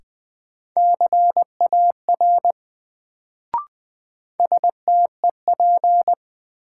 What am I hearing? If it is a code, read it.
Morse code, heard as a keyed tone, about 20 wpm, 700 Hz. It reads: CAR STEP
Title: Sets of 5 Words Encore - New Word Drill - 20wpm